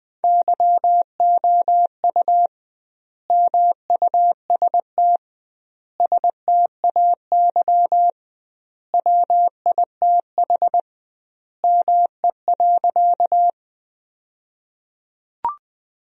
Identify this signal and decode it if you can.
Morse code, heard as a keyed tone, about 20 wpm, 700 Hz. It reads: YOU MUST STAY WITH ME.
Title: 2x Word-Spacing - Sentences from Top 500 Words - 20wpm